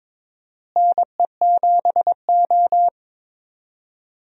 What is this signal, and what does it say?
Morse code, heard as a keyed tone, about 22 wpm, 700 Hz. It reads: NE7O
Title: USA Call Signs - 4 Characters - Repeated 2x - 22wpm